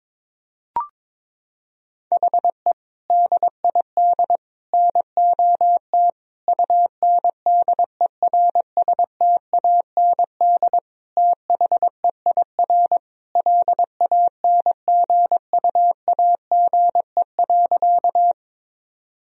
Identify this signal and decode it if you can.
Morse code, heard as a keyed tone, about 22 wpm, 700 Hz. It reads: HE DID NOT UNDERSTAND THEIR LANGUAGE.
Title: Sentences from Top 1000 Words - Review Part 3 - 22wpm